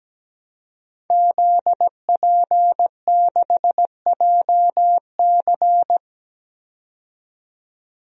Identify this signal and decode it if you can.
Morse code, heard as a keyed tone, about 17 wpm, 700 Hz. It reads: ZP6JC
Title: DXCC-100 Call Signs - 5 Characters - 17wpm